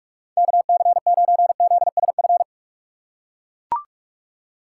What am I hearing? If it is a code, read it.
Morse code, heard as a keyed tone, about 43 wpm, 700 Hz. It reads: KC9ZSF
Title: USA Call Signs - 6 Characters - 4x Character Spacing - 45wpm